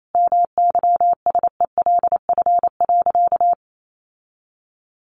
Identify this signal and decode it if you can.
Morse code, heard as a keyed tone, about 28 wpm, 700 Hz. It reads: MYSELF.
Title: Sentences from Top 1300 Words 28wpm